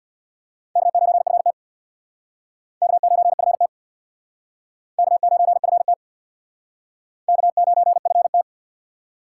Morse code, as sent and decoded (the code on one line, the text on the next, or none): -.- ----. ..-. - / -.- ----. ..-. - / -.- ----. ..-. - / -.- ----. ..-. -
K9FT K9FT K9FT K9FT